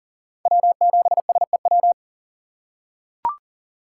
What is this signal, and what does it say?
Morse code, heard as a keyed tone, about 40 wpm, 700 Hz. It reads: W7SEW
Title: USA Call Signs - 5 Characters - 3x Character Spacing - 40wpm